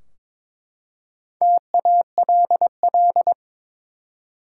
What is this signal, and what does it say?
Morse code, heard as a keyed tone, about 22 wpm, 700 Hz. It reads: TALL
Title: Four Letter Words - 22wpm